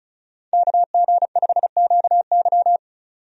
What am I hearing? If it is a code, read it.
Morse code, heard as a keyed tone, about 36 wpm, 700 Hz. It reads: KG5QY
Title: USA Call Signs - 5 Characters - 35wpm